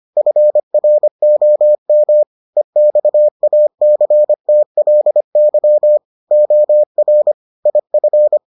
Morse code, heard as a keyed tone, about 25 wpm, 600 Hz. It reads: FROM EXACTLY OR IF